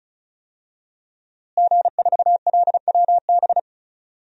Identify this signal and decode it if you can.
Morse code, heard as a keyed tone, about 34 wpm, 700 Hz. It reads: G4LWB
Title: DXCC-100 Call Signs - 5 Characters - Repeated 3x - 35wpm